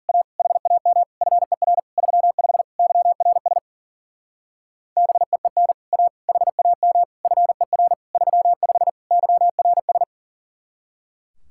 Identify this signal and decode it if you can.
Morse code, heard as a keyed tone, about 50 wpm, 700 Hz. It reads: A HAM FER 35 YRS BEEN A HAM FER 35 YRS